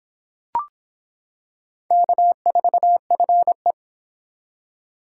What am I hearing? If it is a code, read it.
Morse code, heard as a keyed tone, about 25 wpm, 700 Hz. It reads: K4FE